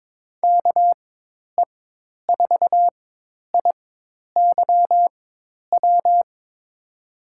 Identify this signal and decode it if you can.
Morse code, heard as a keyed tone, about 21 wpm, spaced as for 10 wpm, 700 Hz. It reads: KE4IYW
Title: USA Call Signs - 6 Characters - 4x Character Spacing - 22wpm